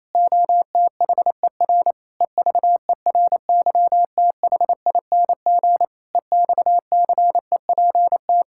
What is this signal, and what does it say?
Morse code, heard as a keyed tone, about 28 wpm, 700 Hz. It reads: OTHER EVERYTHING EXCEPT
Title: Sentences from Top 1600 Words 28wpm